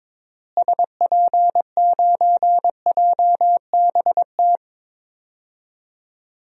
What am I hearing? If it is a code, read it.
Morse code, heard as a keyed tone, about 22 wpm, 700 Hz. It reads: SP9JBT
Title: DXCC-100 Call Signs - 6 Characters - Repeated 3x - 22wpm